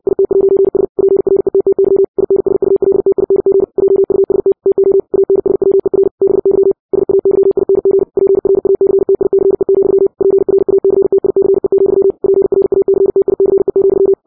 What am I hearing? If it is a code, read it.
Morse code, heard as a keyed tone, about 60 wpm, 390 Hz. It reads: IT'S PRETTY INSULTING JUST TO ANSWER BY SAYING PRACTICE, PRACTICE, PRACTICE,
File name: 60wpm